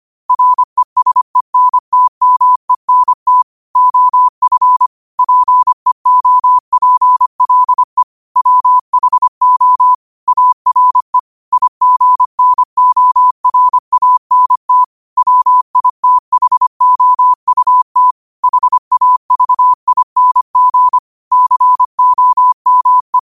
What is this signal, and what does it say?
Morse code, heard as a keyed tone, about 25 wpm, 1000 Hz. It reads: RESENTMENT OF PEOPLE WHO ARE IGNORANT WITHOUT HAVING COME
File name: 25wpm